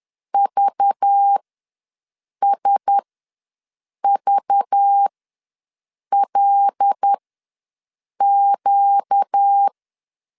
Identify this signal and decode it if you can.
Morse code, heard as a keyed tone, about 10 wpm, spaced as for 6 wpm, 800 Hz. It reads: VSVLQ
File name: LEZIONI / 5 LBVC / 5 LBV ws 10wpm 3645